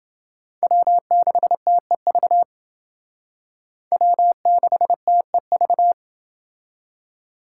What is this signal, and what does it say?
Morse code, heard as a keyed tone, about 28 wpm, 700 Hz. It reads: W6TEV W6TEV